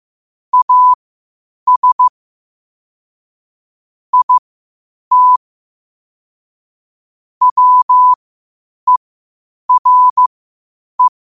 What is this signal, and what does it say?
Morse code, heard as a keyed tone, about 15 wpm, spaced as for 8 wpm, 1000 Hz. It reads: AS IT WERE
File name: qotdfiles / 8wpm